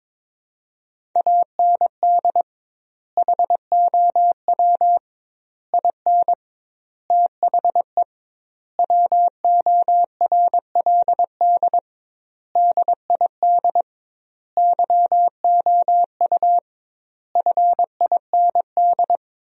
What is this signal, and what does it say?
Morse code, heard as a keyed tone, about 22 wpm, 700 Hz. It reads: AND HOW IN THE WORLD DID YOU FIND
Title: 2x Word-Spacing - Sentences from Top 200 Words - 22wpm